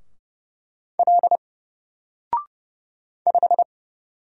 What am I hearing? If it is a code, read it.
Morse code, heard as a keyed tone, about 30 wpm, 700 Hz. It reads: L 5
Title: Single Letter-Number Alt 30wpm